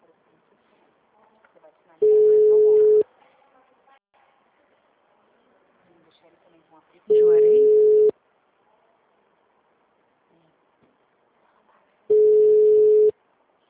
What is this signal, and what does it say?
Ringback tone (Europe, 425 Hz), with speech.